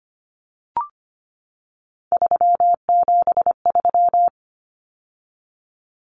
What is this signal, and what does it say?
Morse code, heard as a keyed tone, about 25 wpm, 700 Hz. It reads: 373
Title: Numbers - 3-Digit 25wpm